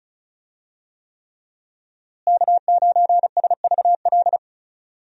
Morse code, heard as a keyed tone, about 36 wpm, 700 Hz. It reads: K9SVL
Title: USA Call Signs - 5 Characters - Repeated 3x - 35wpm